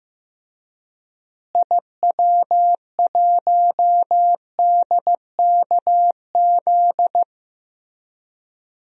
Morse code, heard as a keyed tone, about 15 wpm, 700 Hz. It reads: IW1DKZ
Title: DXCC-100 Call Signs - 6 Characters - Repeated 3x - 15wpm